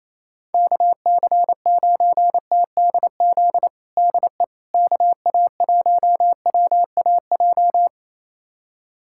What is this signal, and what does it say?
Morse code, heard as a keyed tone, about 28 wpm, 700 Hz. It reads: KC9TDZ DE KA1WAJ
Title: USA Callsign DE Callsign - 6 Characters - 28wpm